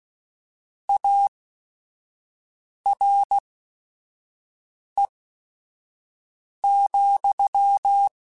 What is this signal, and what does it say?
Morse code, heard as a keyed tone, about 16 wpm, spaced as for 5 wpm, 800 Hz. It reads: ARE,
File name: Technical Subjects / Technical Subjects-mp3s / AND Gate @5